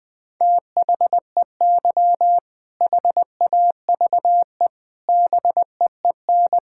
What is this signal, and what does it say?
Morse code, heard as a keyed tone, about 20 wpm, 700 Hz. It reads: THEY HAVE BEEN
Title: Sentences with 6-Word Phrases - 20wpm